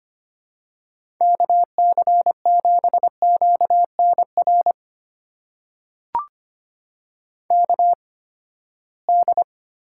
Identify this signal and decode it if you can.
Morse code, heard as a keyed tone, about 25 wpm, spaced as for 17 wpm, 700 Hz. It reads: KC7QNR KD
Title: USA Call Signs - 6 Characters - 8x Character Spacing - 25wpm